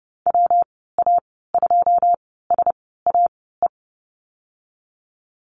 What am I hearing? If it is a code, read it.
Morse code, heard as a keyed tone, about 30 wpm, spaced as for 17 wpm, 700 Hz. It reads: WA2SAE
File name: Lesson 19g - Letter w - Call Signs 30wpm